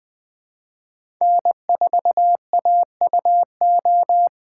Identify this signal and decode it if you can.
Morse code, heard as a keyed tone, about 20 wpm, 700 Hz. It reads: N4AUO
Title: USA Call Signs - 5 Characters - 3x Character Spacing - 20wpm